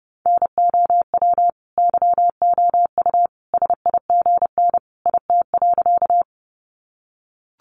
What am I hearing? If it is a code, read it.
Morse code, heard as a keyed tone, about 30 wpm, 700 Hz. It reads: NOW YOU SIGN IT.
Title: Sentences from Top 1000 Words 30wpm